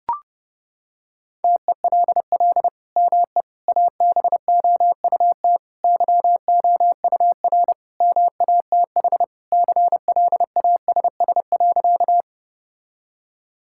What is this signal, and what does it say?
Morse code, heard as a keyed tone, about 30 wpm, 700 Hz. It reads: TELL ME ABOUT YOUR MATH CLASS.
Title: Sentences from Top 1000 Words - Review Part 6 - 30wpm